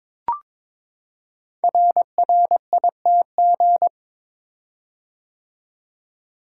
Morse code, heard as a keyed tone, about 23 wpm, 690 Hz. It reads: RRITG
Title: Code Groups - 5 Letter:Numbers - 22wpm